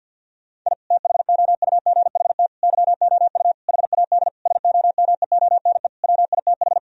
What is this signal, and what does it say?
Morse code, heard as a keyed tone, about 51 wpm, 700 Hz. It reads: I THOUGHT YOU HAD SOMEONE WITH
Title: Sentences from Top 1000 Words - Review Part 4 - 50wpm